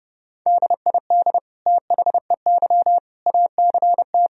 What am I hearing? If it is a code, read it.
Morse code, heard as a keyed tone, about 30 wpm, 700 Hz. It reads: DID THEY ACT
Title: Sentences from Top 1000 Words - Review Part 4 - 30wpm